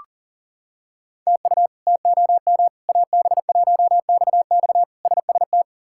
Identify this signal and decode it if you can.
Morse code, heard as a keyed tone, about 40 wpm, 700 Hz. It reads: TU TOM AB1XX SST
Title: K1USN SST QSO Practice - 40wpm